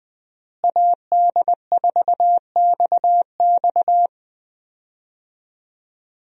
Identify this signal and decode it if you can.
Morse code, heard as a keyed tone, about 20 wpm, 700 Hz. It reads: AD4XX
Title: USA Call Signs - 5 Characters - Repeated 3x - 20wpm